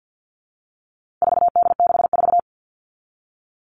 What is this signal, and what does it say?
Morse code, heard as a keyed tone, about 50 wpm, 700 Hz. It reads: 4D64
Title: All Cut Numbers - 4-Digit 50wpm